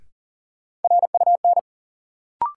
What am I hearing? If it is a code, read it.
Morse code, heard as a keyed tone, about 38 wpm, 700 Hz. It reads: RUN
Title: Sentences from Top 300 Words - New Word Drill - 40wpm